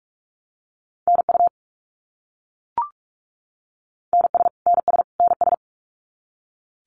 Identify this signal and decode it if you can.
Morse code, heard as a keyed tone, about 44 wpm, 700 Hz. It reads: NU NS NS NS
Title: Canadian Provinces and Territories - Repeated 3x - 45wpm